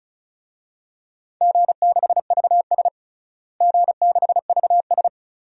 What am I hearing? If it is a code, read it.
Morse code, heard as a keyed tone, about 36 wpm, 690 Hz. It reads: G6VS G6VS
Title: DXCC-100 Call Signs - 4 Characters - Repeated 2x - 35wpm